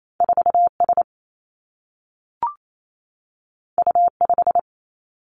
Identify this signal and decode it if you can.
Morse code, heard as a keyed tone, about 28 wpm, 700 Hz. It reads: 4 S U 5
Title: Two Challenging Letter-Number Combinations 28wpm